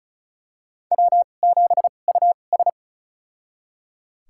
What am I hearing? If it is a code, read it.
Morse code, heard as a keyed tone, about 34 wpm, spaced as for 25 wpm, 700 Hz. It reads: W7US
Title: USA Call Signs - 4 Characters - 2x Character Spacing - 35wpm